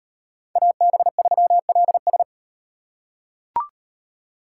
Morse code, heard as a keyed tone, about 38 wpm, 700 Hz. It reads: AB3LS